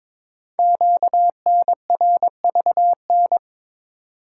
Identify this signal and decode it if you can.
Morse code, heard as a keyed tone, about 22 wpm, 700 Hz. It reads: QNRVN
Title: Code Groups - 5 Letters - 22wpm